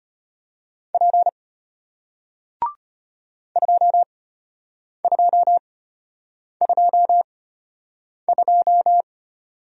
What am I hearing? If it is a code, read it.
Morse code, heard as a keyed tone, about 33 wpm, spaced as for 8 wpm, 700 Hz. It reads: P 2222